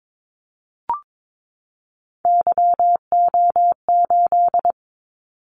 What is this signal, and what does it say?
Morse code, heard as a keyed tone, about 22 wpm, 700 Hz. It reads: YO8
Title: Callsign Prefixes for Top 25 DXCC - 22wpm